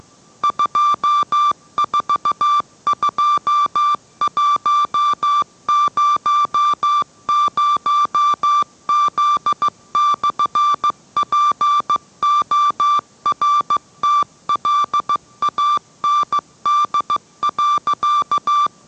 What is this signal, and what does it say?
Morse code, heard as a keyed tone, about 15 wpm, 1170 Hz. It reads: 242100Z/PORTLAND.